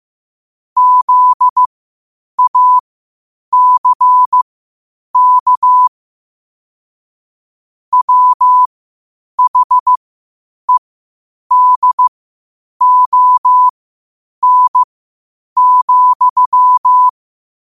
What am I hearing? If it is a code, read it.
Morse code, heard as a keyed tone, about 15 wpm, spaced as for 8 wpm, 1000 Hz. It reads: ZACK WHEDON,
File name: qotdfiles / 8wpm